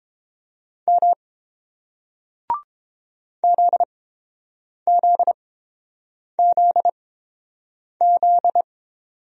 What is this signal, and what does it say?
Morse code, heard as a keyed tone, about 30 wpm, spaced as for 7 wpm, 700 Hz. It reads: M ZZZZ